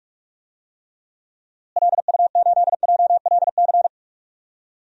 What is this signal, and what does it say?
Morse code, heard as a keyed tone, about 43 wpm, 700 Hz. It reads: LU8JLC
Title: DXCC-100 Call Signs - 6 Characters - Repeated 2x - 45wpm